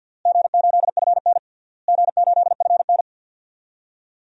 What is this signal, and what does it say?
Morse code, heard as a keyed tone, about 52 wpm, 700 Hz. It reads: G8FN G8FN
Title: DXCC-100 Call Signs - 4 Characters - Repeated 3x - 50wpm